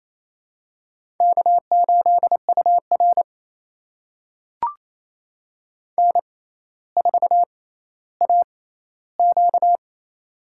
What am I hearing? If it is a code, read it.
Morse code, heard as a keyed tone, about 28 wpm, spaced as for 14 wpm, 700 Hz. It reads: K8UR N4AQ
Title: USA Call Signs - 4 Characters - 6x Character Spacing - 28wpm